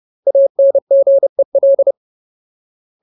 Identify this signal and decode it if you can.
Morse code, heard as a keyed tone, about 30 wpm, 550 Hz. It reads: ANGEL
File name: Ultimate Word List / 30WPM Ultimate